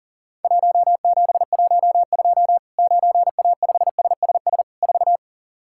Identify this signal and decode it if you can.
Morse code, heard as a keyed tone, about 40 wpm, 700 Hz. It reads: 1712 9A5SSS 4
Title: Worked All Europe DX Contest - QTC Exchange - 40wpm